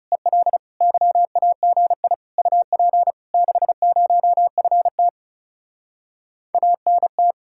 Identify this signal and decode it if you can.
Morse code, heard as a keyed tone, about 33 wpm, 700 Hz. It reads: EL YAGI UP 60FT ANT
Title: Speed-Racing Common QSO Phrases - 30wpm